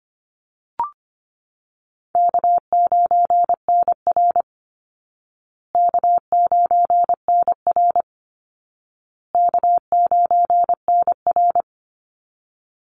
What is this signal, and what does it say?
Morse code, heard as a keyed tone, about 25 wpm, 700 Hz. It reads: K9NR K9NR K9NR
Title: Easy Call Signs Repeated 3x - 25wpm